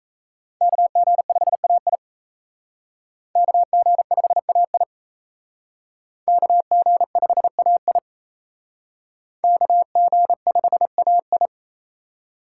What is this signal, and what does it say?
Morse code, heard as a keyed tone, about 35 wpm, 700 Hz. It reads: KG5AI KG5AI KG5AI KG5AI
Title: Speed Racing - USA 2x2 Call Signs 28wpm